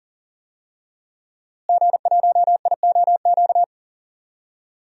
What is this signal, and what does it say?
Morse code, heard as a keyed tone, about 40 wpm, 700 Hz. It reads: G1IOQ
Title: DXCC-100 Call Signs - 5 Characters - Repeated 3x - 40wpm